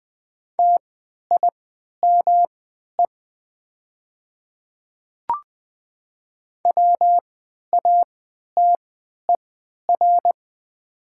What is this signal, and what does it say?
Morse code, heard as a keyed tone, about 20 wpm, spaced as for 11 wpm, 700 Hz. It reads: TIME WATER
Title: 3x Character Spacing - Top 100 Words - 20wpm